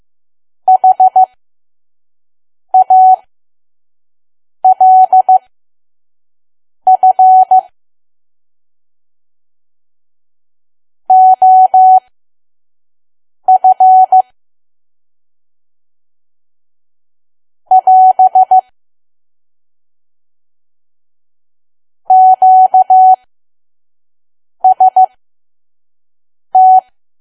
Morse code, heard as a keyed tone, about 15 wpm, spaced as for 5 wpm, 750 Hz. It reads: HALF OF <AS> QST
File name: Archive / 5 WPM files / PM